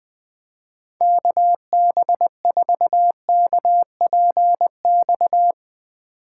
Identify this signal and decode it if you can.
Morse code, heard as a keyed tone, about 22 wpm, 700 Hz. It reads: KB4KPX
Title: Speed Racing - USA 2x3 Call Signs 20wpm